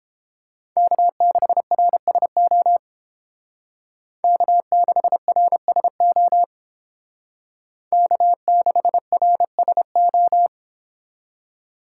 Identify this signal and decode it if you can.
Morse code, heard as a keyed tone, about 30 wpm, 700 Hz. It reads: K6RSO K6RSO K6RSO